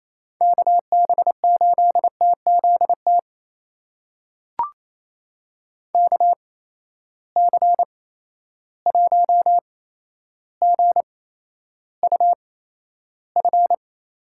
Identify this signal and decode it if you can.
Morse code, heard as a keyed tone, about 28 wpm, spaced as for 12 wpm, 700 Hz. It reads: KB8TZT KC1GUF
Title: USA Call Signs - 6 Characters - 8x Character Spacing - 28wpm